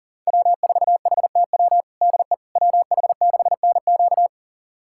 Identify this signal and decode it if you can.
Morse code, heard as a keyed tone, about 40 wpm, 700 Hz. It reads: W4HTW DE WH6NQ
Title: USA Callsign DE Callsign - 5 Characters - 40wpm